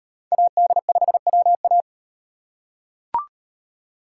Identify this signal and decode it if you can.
Morse code, heard as a keyed tone, about 38 wpm, 700 Hz. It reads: AD5WA